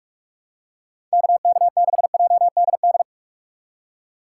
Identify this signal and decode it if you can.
Morse code, heard as a keyed tone, about 46 wpm, 700 Hz. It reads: KK6JDD
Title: USA Call Signs - 6 Characters - 2x Character Spacing - 45wpm